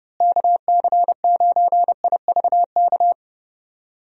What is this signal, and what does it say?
Morse code, heard as a keyed tone, about 30 wpm, 700 Hz. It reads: KC9IVK
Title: USA Call Signs - 6 Characters - Repeated 3x - 30wpm